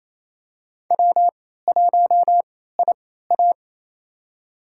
Morse code, heard as a keyed tone, about 27 wpm, spaced as for 16 wpm, 700 Hz. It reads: W1IA
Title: USA Call Signs - 4 Characters - 3x Character Spacing - 28wpm